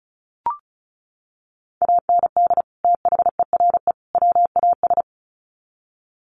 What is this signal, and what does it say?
Morse code, heard as a keyed tone, about 34 wpm, 700 Hz. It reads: AND THERE WAS
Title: Sets of 3 Words 35wpm